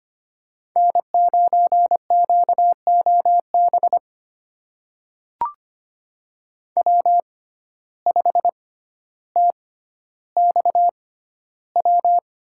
Morse code, heard as a keyed tone, about 25 wpm, spaced as for 13 wpm, 700 Hz. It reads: N9QOB W5TXW
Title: USA Call Signs - 5 Characters - 6x Character Spacing - 25wpm